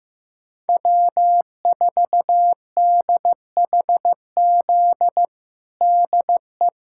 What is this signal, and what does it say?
Morse code, heard as a keyed tone, about 15 wpm, 700 Hz. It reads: W4DHZ DE
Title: USA Callsign DE Callsign - 5 Characters - 15wpm